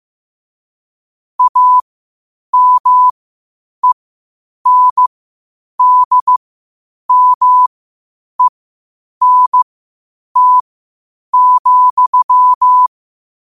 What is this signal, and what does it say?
Morse code, heard as a keyed tone, about 15 wpm, spaced as for 8 wpm, 1000 Hz. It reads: AMENDMENT,
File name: qotdfiles / 8wpm